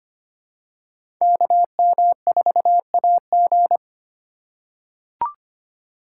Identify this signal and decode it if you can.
Morse code, heard as a keyed tone, about 25 wpm, 700 Hz. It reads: KM4AG
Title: USA Call Signs - 5 Characters - 3x Character Spacing - 25wpm